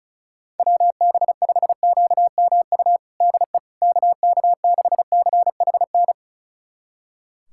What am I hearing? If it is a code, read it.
Morse code, heard as a keyed tone, about 36 wpm, 700 Hz. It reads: WB5QMU DE KK6CHN
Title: USA Callsign DE Callsign 35wpm